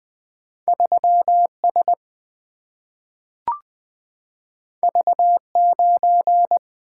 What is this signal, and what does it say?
Morse code, heard as a keyed tone, about 20 wpm, 700 Hz. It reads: 3S V9
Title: Single-Letter Single-Number 20wpm